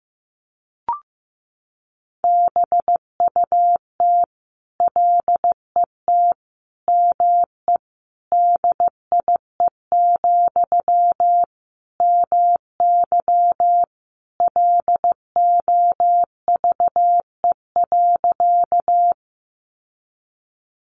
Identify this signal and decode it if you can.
Morse code, heard as a keyed tone, about 15 wpm, 700 Hz. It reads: BUT LET ME DIE, MY LOVE.
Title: Sentences from Top 1300 Words 15wpm